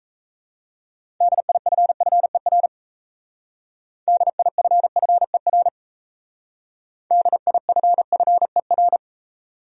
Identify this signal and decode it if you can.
Morse code, heard as a keyed tone, about 36 wpm, 700 Hz. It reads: DIFFER DIFFER DIFFER